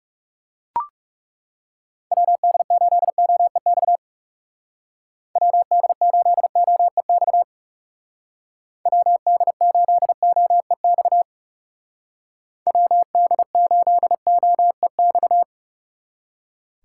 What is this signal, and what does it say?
Morse code, heard as a keyed tone, about 40 wpm, 700 Hz. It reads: WD8OEX WD8OEX WD8OEX WD8OEX